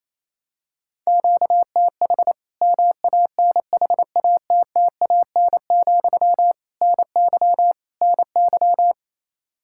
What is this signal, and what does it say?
Morse code, heard as a keyed tone, about 28 wpm, 700 Hz. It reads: QTH MANHATTAN, NY NY
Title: Practice QSO - Exchange 1 - Beginning - 28wpm